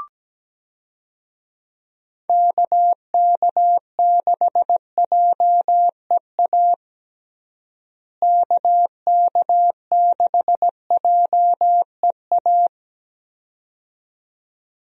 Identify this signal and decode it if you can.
Morse code, heard as a keyed tone, about 17 wpm, 700 Hz. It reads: KK6JEA KK6JEA
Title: USA Call Signs - 6 Characters - Repeated 2x - 17wpm